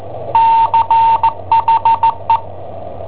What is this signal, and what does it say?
Morse code, heard as a keyed tone, about 16 wpm, 910 Hz. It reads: CHE
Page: Cassetta esercizi preparata da I7OHP per IW5ELP